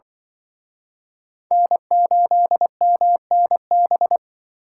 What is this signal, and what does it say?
Morse code, heard as a keyed tone, about 25 wpm, 700 Hz. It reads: N8MNB
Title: Speed Racing - USA 1x3 Call Signs 20wpm